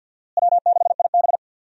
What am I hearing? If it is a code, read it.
Morse code, heard as a keyed tone, about 53 wpm, 700 Hz. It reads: W6IB
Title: USA Call Signs - 4 Characters - Repeated 3x - 50wpm